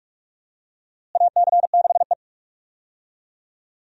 Morse code, heard as a keyed tone, about 46 wpm, 700 Hz. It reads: AC6E
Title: USA Call Signs - 4 Characters - Repeated 2x - 45wpm